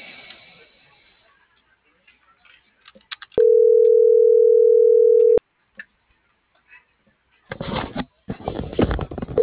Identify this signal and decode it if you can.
Ringback tone (North America), 440 + 480 Hz.